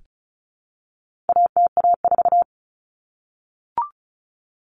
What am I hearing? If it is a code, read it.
Morse code, heard as a keyed tone, about 34 wpm, 700 Hz. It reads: ATA4
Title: Common Cut Numbers - 4-Digit 35wpm